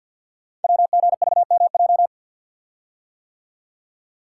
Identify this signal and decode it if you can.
Morse code, heard as a keyed tone, about 47 wpm, 700 Hz. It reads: WGVMJ
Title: Code Groups - 5 Letters - 50wpm